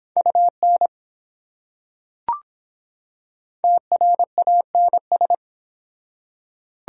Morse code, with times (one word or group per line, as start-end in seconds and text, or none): 0.2-0.9 UN
3.6-5.3 TRANS